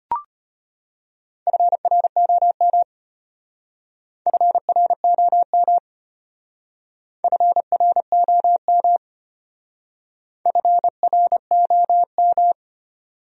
Morse code, heard as a keyed tone, about 33 wpm, 700 Hz. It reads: FROM FROM FROM FROM